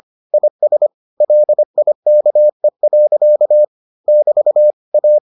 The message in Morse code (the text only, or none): IS LIKE. <BT> A